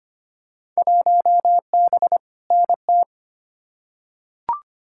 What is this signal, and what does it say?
Morse code, heard as a keyed tone, about 25 wpm, 700 Hz. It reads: 1B NT
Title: Field Day Exchanges Encore - 25wpm